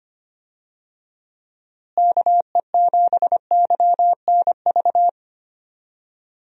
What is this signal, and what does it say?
Morse code, heard as a keyed tone, about 25 wpm, 700 Hz. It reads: KE7YNV
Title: USA Call Signs - 6 Characters - Repeated 3x - 25wpm